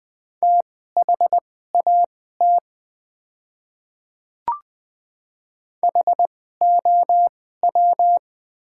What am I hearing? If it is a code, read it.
Morse code, heard as a keyed tone, about 20 wpm, spaced as for 14 wpm, 700 Hz. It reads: THAT HOW
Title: 2x Character Spacing - Top 100 Words - 20wpm